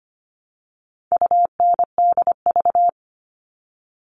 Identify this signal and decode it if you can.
Morse code, heard as a keyed tone, about 25 wpm, 700 Hz. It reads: UNDV
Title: All Cut Numbers - 4-Digit 25wpm